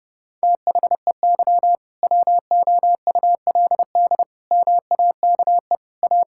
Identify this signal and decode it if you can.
Morse code, heard as a keyed tone, about 30 wpm, 700 Hz. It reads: THEY WOULD MAKE A
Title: Sentences from Top 1600 Words 30wpm